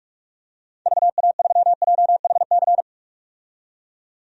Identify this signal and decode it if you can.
Morse code, heard as a keyed tone, about 41 wpm, 700 Hz. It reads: VA3JHC
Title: DXCC-100 Call Signs - 6 Characters - 45wpm